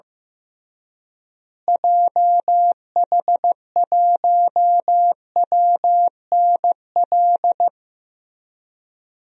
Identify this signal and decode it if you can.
Morse code, heard as a keyed tone, about 15 wpm, 700 Hz. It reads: JH1WNL
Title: DXCC-100 Call Signs - 6 Characters - Repeated 3x - 15wpm